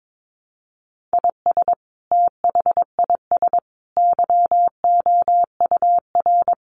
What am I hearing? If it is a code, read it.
Morse code, heard as a keyed tone, about 22 wpm, 700 Hz. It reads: IS THIS YOUR
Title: Sentences from Top 1500 Words 22wpm